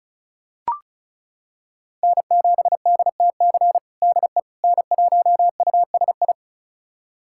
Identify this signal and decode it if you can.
Morse code, heard as a keyed tone, about 36 wpm, 700 Hz. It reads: N7DTC DE N1USI
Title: USA Callsign DE Callsign - 5 Characters - 35wpm